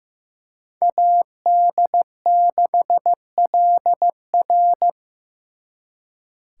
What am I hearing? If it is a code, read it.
Morse code, heard as a keyed tone, about 18 wpm, 700 Hz. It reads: AD6LR
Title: Speed Racing - USA 2x2 Call Signs 15wpm